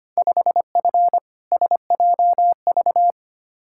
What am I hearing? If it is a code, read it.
Morse code, heard as a keyed tone, about 25 wpm, 700 Hz. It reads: 5F SJV
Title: Field Day Exchanges Encore - 25wpm